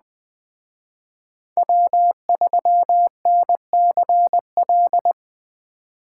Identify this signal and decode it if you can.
Morse code, heard as a keyed tone, about 20 wpm, 700 Hz. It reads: W3NCL
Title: Speed Racing - USA 1x3 Call Signs 17wpm